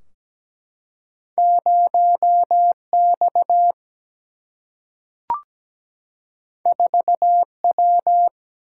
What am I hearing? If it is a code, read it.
Morse code, heard as a keyed tone, about 17 wpm, 700 Hz. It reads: 0X 4W
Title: Single-Letter Single-Number 17wpm